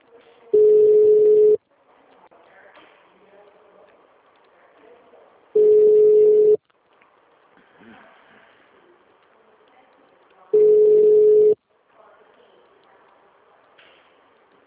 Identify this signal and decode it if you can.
Ringback tone (Europe, 425 Hz), with speech.